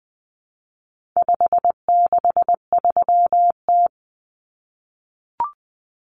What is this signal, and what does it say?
Morse code, heard as a keyed tone, about 20 wpm, 700 Hz. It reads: 563T
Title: Common Cut Numbers - 4-Digit 20wpm